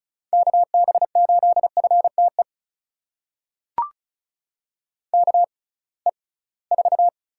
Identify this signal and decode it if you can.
Morse code, heard as a keyed tone, about 36 wpm, spaced as for 23 wpm, 700 Hz. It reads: KB8FTE KE4
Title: USA Call Signs - 6 Characters - 6x Character Spacing - 35wpm